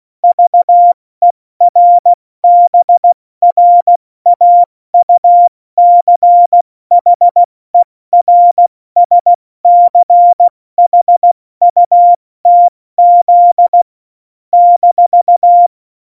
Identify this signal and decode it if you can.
Morse code, heard as a keyed tone, about 16 wpm, 700 Hz. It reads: VERBRAUCHERSCHUTZ -
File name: Morsekurs / Witze / Witz 14 Juli 2020, 70 F80